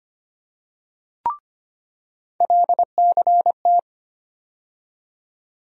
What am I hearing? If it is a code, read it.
Morse code, heard as a keyed tone, about 25 wpm, 700 Hz. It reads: LCT
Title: USA Call Sign Postfixes - 3 Letters 25wpm